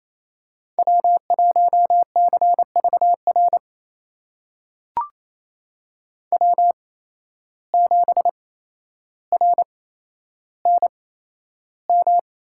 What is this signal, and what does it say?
Morse code, heard as a keyed tone, about 28 wpm, spaced as for 12 wpm, 700 Hz. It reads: W1CVR W7RNM
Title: USA Call Signs - 5 Characters - 8x Character Spacing - 28wpm